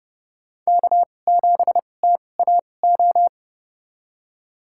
Morse code, heard as a keyed tone, about 30 wpm, spaced as for 22 wpm, 700 Hz. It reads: K7TAO
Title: USA Call Signs - 5 Characters - 2x Character Spacing - 30wpm